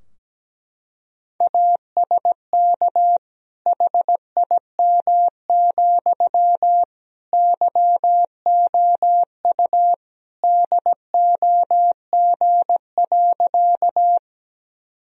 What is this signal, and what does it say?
Morse code, heard as a keyed tone, about 17 wpm, 700 Hz. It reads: ASK HIM, YOU DOG.
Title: Sentences from Top 1000 Words - Review Part 2 - 17wpm